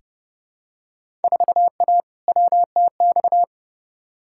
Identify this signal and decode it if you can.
Morse code, heard as a keyed tone, about 30 wpm, 700 Hz. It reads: 4A WTX
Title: Field Day Exchanges - 30wpm